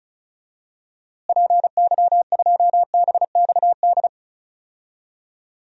Morse code, heard as a keyed tone, about 36 wpm, 700 Hz. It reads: PY2BXD
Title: DXCC-100 Call Signs - 6 Characters - Repeated 2x - 35wpm